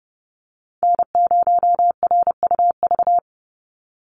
Morse code, heard as a keyed tone, about 30 wpm, 700 Hz. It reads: N0RUV
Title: USA Call Signs 30wpm